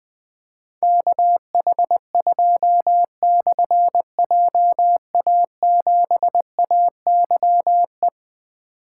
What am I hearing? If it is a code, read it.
Morse code, heard as a keyed tone, about 20 wpm, 700 Hz. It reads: KH2/JA7AYE
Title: Difficult Call Signs 20wpm